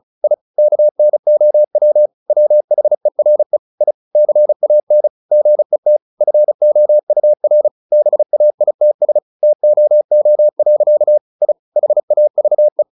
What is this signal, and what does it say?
Morse code, heard as a keyed tone, about 34 wpm, 600 Hz. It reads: I KNOW WHERE I CAN GET FOUR BAITS TOO. I HAVE
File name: the old man and the sea / the old man and the sea 1 35wpm